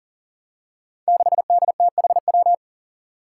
Morse code, heard as a keyed tone, about 39 wpm, 700 Hz. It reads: 6DTHW
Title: Five Challenging Letter-Number Combinations 40wpm